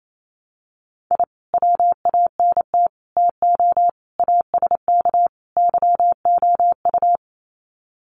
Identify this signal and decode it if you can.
Morse code, heard as a keyed tone, about 28 wpm, 700 Hz. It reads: I WANT TO ASK YOU
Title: Sets of 5 Words 28wpm